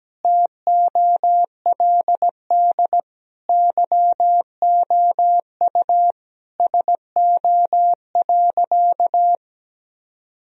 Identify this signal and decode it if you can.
Morse code, heard as a keyed tone, about 17 wpm, 700 Hz. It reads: TOLD YOU SO.
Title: Sentences from Top 1000 Words - Review Part 1 - 17wpm